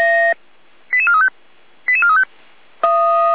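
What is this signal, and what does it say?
Five-tone selective call (ZVEI), heard as quick runs of five tones, tones of about 70 ms. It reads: 88435 88435
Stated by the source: ZVEI five-tone selective call 88335, received on an SDR, from the Feuerwehr Orlamünde (fire brigade) audio server.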